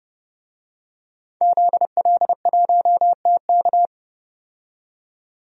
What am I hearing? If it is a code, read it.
Morse code, heard as a keyed tone, about 31 wpm, 700 Hz. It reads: ZL1TK
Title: DXCC-100 Call Signs - 5 Characters - 30wpm